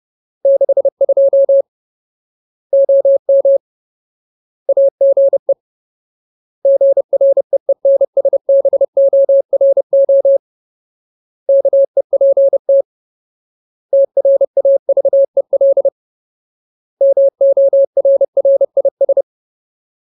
Morse code, heard as a keyed tone, about 30 wpm, 550 Hz. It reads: 62 OM AGE GREENSBORO KEPT TRAVEL MORRIS
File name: Ultimate Word List / 30WPM Ultimate